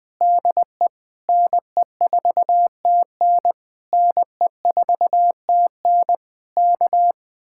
Morse code, heard as a keyed tone, about 20 wpm, 700 Hz. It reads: DE NE4TN NE4TN K
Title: Speed-Racing Common QSO Phrases - 17wpm